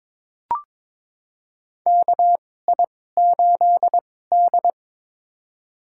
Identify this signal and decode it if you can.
Morse code, heard as a keyed tone, about 22 wpm, spaced as for 16 wpm, 700 Hz. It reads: KI8D
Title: USA Call Signs - 4 Characters - 2x Character Spacing - 22wpm